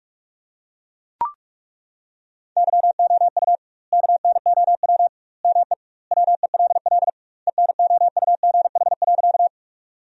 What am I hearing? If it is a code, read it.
Morse code, heard as a keyed tone, about 42 wpm, 700 Hz. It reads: YOU KNOW ME WELL ENOUGH.
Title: 2x Word-Spacing - Sentences from Top 400 Words - 45wpm